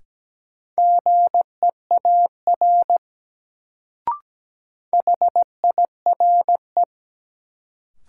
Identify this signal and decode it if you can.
Morse code, heard as a keyed tone, about 17 wpm, 700 Hz. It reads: GEAR HIRE
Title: Sentences from Top 1400 Words - New Word Drill - 17wpm